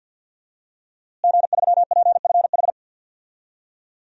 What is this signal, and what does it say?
Morse code, heard as a keyed tone, about 52 wpm, 700 Hz. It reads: G3PFH
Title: DXCC-100 Call Signs - 5 Characters - Repeated 3x - 50wpm